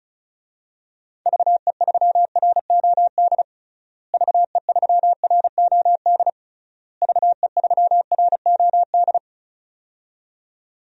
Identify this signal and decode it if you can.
Morse code, heard as a keyed tone, about 36 wpm, 700 Hz. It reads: VE3ROD VE3ROD VE3ROD
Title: DXCC-100 Call Signs - 6 Characters - Repeated 3x - 35wpm